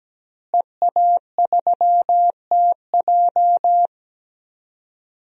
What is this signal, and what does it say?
Morse code, heard as a keyed tone, about 17 wpm, 700 Hz. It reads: EA3TJ
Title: DXCC-100 Call Signs - 5 Characters - Repeated 3x - 17wpm